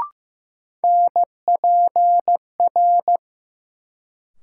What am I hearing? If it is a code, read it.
Morse code, heard as a keyed tone, about 15 wpm, 700 Hz. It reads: NPR
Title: Sets of 5 Words - New Word Drill - 15wpm